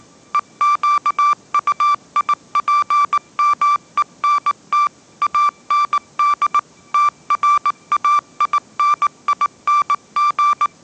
Morse code, heard as a keyed tone, about 19 wpm, 1170 Hz. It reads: EQUIPMENT AND TRAINING
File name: 20wpm